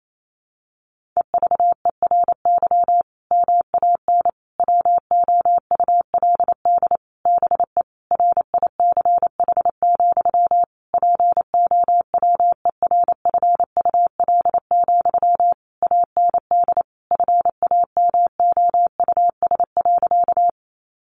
Morse code, heard as a keyed tone, about 28 wpm, 700 Hz. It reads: EVERY MAN WOULD BE RICH, POWERFUL, AND FAMOUS.
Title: Sentences from Top 1500 Words 28wpm